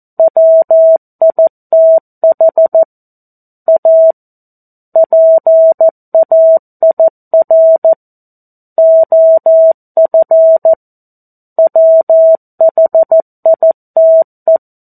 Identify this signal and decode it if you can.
Morse code, heard as a keyed tone, about 14 wpm, 650 Hz. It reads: WITH A PAIR OF WHITE
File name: 02-Chapter-02 14 WPM